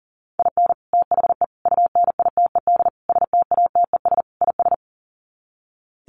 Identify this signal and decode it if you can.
Morse code, heard as a keyed tone, about 40 wpm, 700 Hz. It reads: IN THE UNITED STATES IS
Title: Sets of 5 Words 40wpm